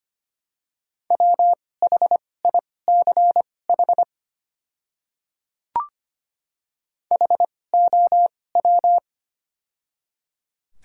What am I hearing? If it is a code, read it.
Morse code, heard as a keyed tone, about 25 wpm, spaced as for 18 wpm, 700 Hz. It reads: WHICH HOW
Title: 2x Character Spacing - Top 100 Words - 25wpm